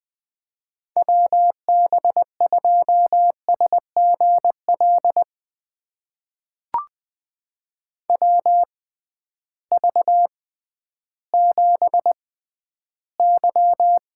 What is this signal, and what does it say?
Morse code, heard as a keyed tone, about 20 wpm, spaced as for 12 wpm, 700 Hz. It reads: WB2SGL WV7Y
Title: USA Call Signs - 6x Character Spacing - 20wpm